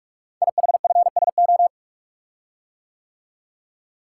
Morse code, heard as a keyed tone, about 44 wpm, 700 Hz. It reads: IHFSO
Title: Code Groups - 5 Letters - 45wpm